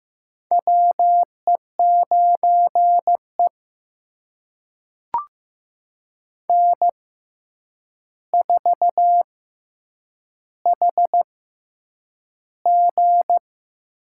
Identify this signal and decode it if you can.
Morse code, heard as a keyed tone, about 15 wpm, spaced as for 8 wpm, 700 Hz. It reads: WE9E N4HG
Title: USA Call Signs - 4 Characters - 6x Character Spacing - 15wpm